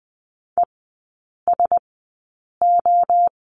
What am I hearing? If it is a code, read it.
Morse code, heard as a keyed tone, about 20 wpm, spaced as for 8 wpm, 700 Hz. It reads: ESO
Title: 041 Lesson 8e - Number 1 - Three Characters 20wpm